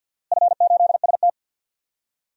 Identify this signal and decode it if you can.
Morse code, heard as a keyed tone, about 48 wpm, 700 Hz. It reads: F8ST
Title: DXCC-100 Call Signs - 4 Characters - Repeated 3x - 50wpm